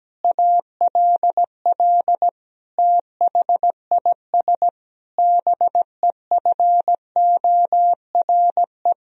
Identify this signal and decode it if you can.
Morse code, heard as a keyed tone, about 17 wpm, 700 Hz. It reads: ALL THIS BEFORE
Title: Sentences from Top 1000 Words - Review Part 4 - 17wpm